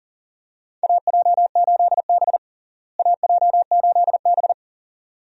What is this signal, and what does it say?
Morse code, heard as a keyed tone, about 40 wpm, 700 Hz. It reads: AJ8B AJ8B
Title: USA Call Signs - 4 Characters - Repeated 2x - 40wpm